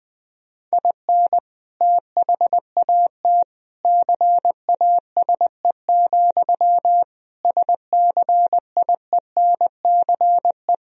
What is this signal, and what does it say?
Morse code, heard as a keyed tone, about 20 wpm, 700 Hz. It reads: IN THAT CASE, SCIENCE
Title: Sentences from Top 1000 Words - Review Part 2 - 20wpm